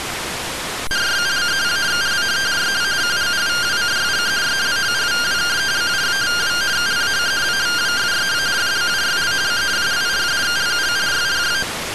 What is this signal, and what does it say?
Two-tone teleprinter signal, about 170 Hz shift.